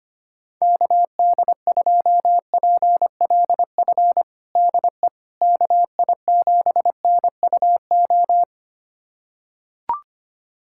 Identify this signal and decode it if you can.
Morse code, heard as a keyed tone, about 25 wpm, 700 Hz. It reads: KD2PLF DE KI7NUO
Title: USA Callsign DE Callsign - 6 Characters - 25wpm